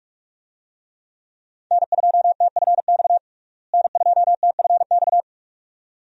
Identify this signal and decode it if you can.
Morse code, heard as a keyed tone, about 42 wpm, 700 Hz. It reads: N2TFX N2TFX
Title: USA Call Signs - 5 Characters - Repeated 2x - 45wpm